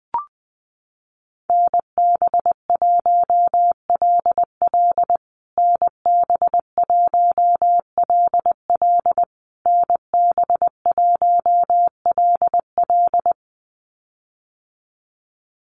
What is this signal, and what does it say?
Morse code, heard as a keyed tone, about 20 wpm, 700 Hz. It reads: NB1LL NB1LL NB1LL
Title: USA Call Signs Repeated 3x 20wpm